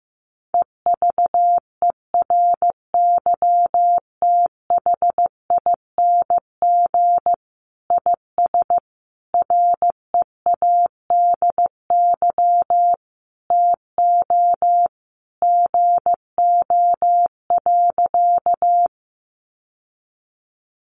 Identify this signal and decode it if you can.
Morse code, heard as a keyed tone, about 14 wpm, 700 Hz. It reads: EVERYTHING IS READY TO GO.
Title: Sentences from Top 700 Words 15wpm